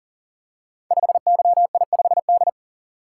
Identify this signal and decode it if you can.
Morse code, heard as a keyed tone, about 40 wpm, 690 Hz. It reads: 5YI5D
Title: Code Groups - 5 Letter:Numbers - 40wpm